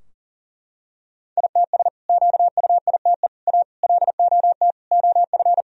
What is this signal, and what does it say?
Morse code, heard as a keyed tone, about 37 wpm, 700 Hz. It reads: ITS QUITE A LOT OF
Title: Sentences from Top 1000 Words - Review Part 3 - 40wpm